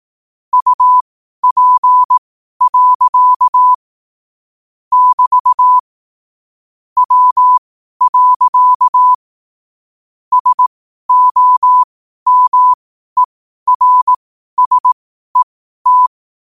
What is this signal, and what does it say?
Morse code, heard as a keyed tone, about 18 wpm, spaced as for 12 wpm, 1000 Hz. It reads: UP. <BT> W. SOMERSET
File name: qotdfiles / 12wpm